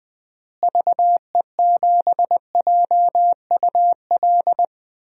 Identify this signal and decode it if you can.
Morse code, heard as a keyed tone, about 20 wpm, 700 Hz. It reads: VE7JUL
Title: DXCC-100 Call Signs - 6 Characters - Repeated 3x - 20wpm